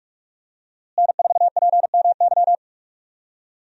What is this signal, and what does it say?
Morse code, heard as a keyed tone, about 44 wpm, 700 Hz. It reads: N4PMY
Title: USA Call Signs - 5 Characters - Repeated 2x - 45wpm